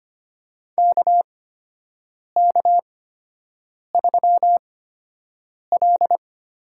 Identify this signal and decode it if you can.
Morse code, heard as a keyed tone, about 25 wpm, spaced as for 7 wpm, 700 Hz. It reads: KK3L
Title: USA Call Signs - 4 Characters - 8x Character Spacing - 25wpm